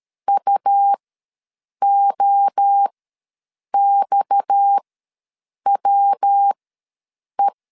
Morse code, heard as a keyed tone, about 13 wpm, spaced as for 7 wpm, 800 Hz. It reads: UOXWE
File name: LEZIONI / 6 PKX0 / 6 PKX ws 12wpm 3106